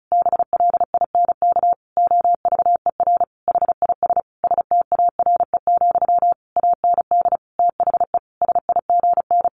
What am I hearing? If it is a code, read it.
Morse code, heard as a keyed tone, about 34 wpm, 700 Hz. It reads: BLINK OVER HIS STARE, AND THE SIGN
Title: A Sound Of Thunder 35wpm